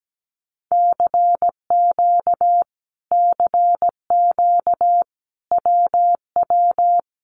Morse code, heard as a keyed tone, about 17 wpm, 700 Hz. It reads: CQ CQ WW
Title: Common QSO Phrases Encore 17wpm